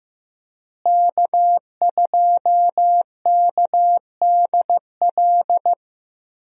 Morse code, heard as a keyed tone, about 15 wpm, 700 Hz. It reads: K2KDL
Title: USA Call Signs - 5 Characters - Repeated 2x - 15wpm